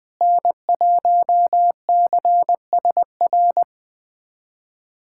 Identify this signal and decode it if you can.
Morse code, heard as a keyed tone, about 21 wpm, 700 Hz. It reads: N1CSR
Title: USA Callsign DE Callsign 20wpm